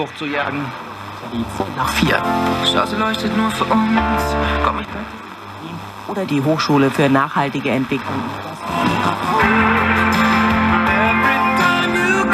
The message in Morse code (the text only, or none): none